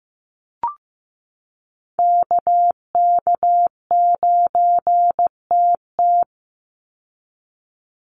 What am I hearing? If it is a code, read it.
Morse code, heard as a keyed tone, about 15 wpm, 700 Hz. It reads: KK9TT
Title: USA Call Signs 15wpm